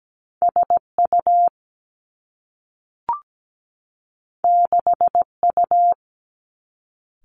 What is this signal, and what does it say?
Morse code, heard as a keyed tone, about 17 wpm, 700 Hz. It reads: SU 6U
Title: Two Challenging Letter-Number Combinations 17wpm